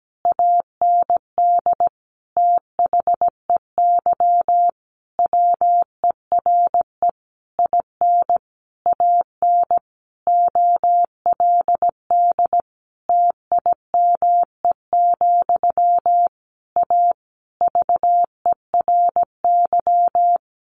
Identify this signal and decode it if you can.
Morse code, heard as a keyed tone, about 17 wpm, 700 Hz. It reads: AND THEY WERE IN AN OLD TIME, A VERY
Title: A Sound Of Thunder 17wpm